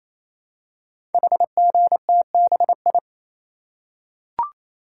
Morse code, heard as a keyed tone, about 29 wpm, 700 Hz. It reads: HGTBI
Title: Code Groups - 5 Letters - 28wpm